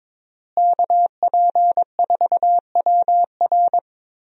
Morse code, heard as a keyed tone, about 22 wpm, 700 Hz. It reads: KP4WR
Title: USA Call Signs - 5 Characters - Repeated 2x - 22wpm